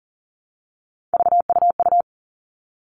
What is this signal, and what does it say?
Morse code, heard as a keyed tone, about 40 wpm, 700 Hz. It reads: VUU
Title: All Cut Numbers - 3-Digit 40wpm